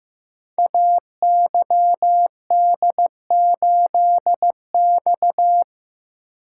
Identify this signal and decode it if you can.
Morse code, heard as a keyed tone, about 15 wpm, 700 Hz. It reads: AYD8X
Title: Code Groups - 5 Letter:Numbers - 15wpm